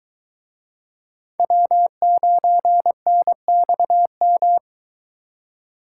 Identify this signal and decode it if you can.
Morse code, heard as a keyed tone, about 23 wpm, 700 Hz. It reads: W9NXM